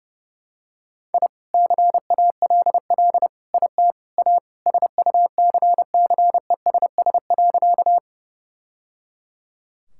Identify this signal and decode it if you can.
Morse code, heard as a keyed tone, about 30 wpm, 700 Hz. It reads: I CALL IT A SUCCESS.
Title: Sentences from Top 1000 Words - Review Part 6 - 30wpm